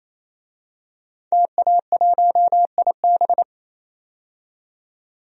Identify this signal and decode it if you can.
Morse code, heard as a keyed tone, about 28 wpm, 700 Hz. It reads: TA1IB
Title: DXCC-100 Call Signs - 5 Characters - 28wpm